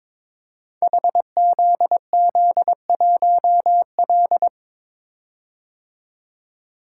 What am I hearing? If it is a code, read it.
Morse code, heard as a keyed tone, about 22 wpm, 700 Hz. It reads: HZZ1L
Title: Code Groups - 5 Letter:Numbers - 22wpm